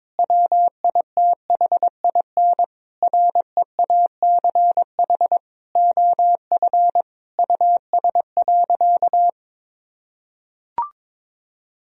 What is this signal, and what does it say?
Morse code, heard as a keyed tone, about 22 wpm, 700 Hz. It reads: WITHIN REACH OF US.
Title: Sentences from Top 1000 Words - Review Part 4 - 22wpm